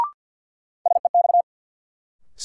Morse code, heard as a keyed tone, about 47 wpm, 700 Hz. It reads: SEX
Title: Sentences from Top 1600 Words - New Word Drill - 50wpm